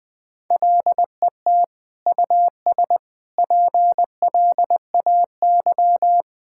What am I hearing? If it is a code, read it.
Morse code, heard as a keyed tone, about 20 wpm, 700 Hz. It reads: LET US PLAY
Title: Sentences from Top 1600 Words 20wpm